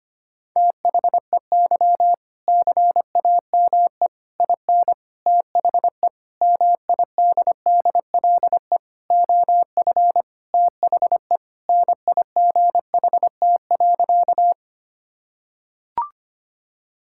Morse code, heard as a keyed tone, about 25 wpm, 700 Hz. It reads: THEY CAME IN THE MIDDLE OF THE NIGHT.
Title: Sentences from Top 1000 Words - Review Part 6 - 25wpm